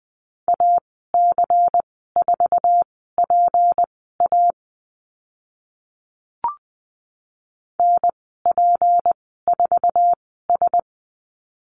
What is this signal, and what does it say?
Morse code, heard as a keyed tone, about 20 wpm, spaced as for 15 wpm, 700 Hz. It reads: AC4PA NP4S
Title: 154 Lesson 25g - Letter p - Call Signs 20wpm